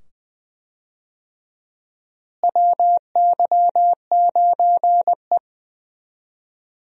Morse code, heard as a keyed tone, about 20 wpm, 700 Hz. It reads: WY9E
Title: USA Call Signs - 4 Characters - Repeated 2x - 20wpm